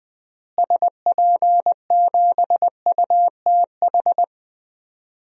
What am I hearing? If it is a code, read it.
Morse code, heard as a keyed tone, about 21 wpm, 700 Hz. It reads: SP7UTH
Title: DXCC-100 Call Signs - 6 Characters - Repeated 2x - 20wpm